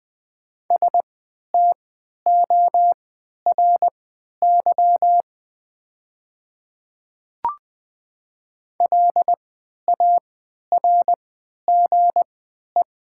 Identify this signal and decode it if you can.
Morse code, heard as a keyed tone, about 20 wpm, spaced as for 11 wpm, 700 Hz. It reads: STORY LARGE
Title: 3x Character Spacing - Top 500 Words - 20wpm